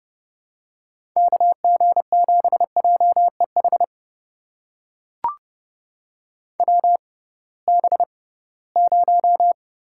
Morse code, heard as a keyed tone, about 30 wpm, spaced as for 19 wpm, 700 Hz. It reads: KG7JEH WB0
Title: USA Call Signs - 6 Characters - 6x Character Spacing - 30wpm